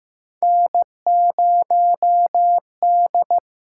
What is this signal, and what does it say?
Morse code, heard as a keyed tone, about 15 wpm, 700 Hz. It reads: N0D
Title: USA Call Signs - 6x Character Spacing - 15wpm